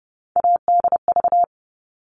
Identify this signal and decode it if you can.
Morse code, heard as a keyed tone, about 30 wpm, 700 Hz. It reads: ADV
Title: All Cut Numbers - 3-Digit 30wpm